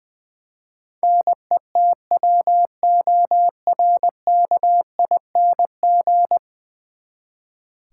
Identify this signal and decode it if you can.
Morse code, heard as a keyed tone, about 20 wpm, 700 Hz. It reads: NETWORKING
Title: Practice QSO - Exchange 3 - More Info - 20wpm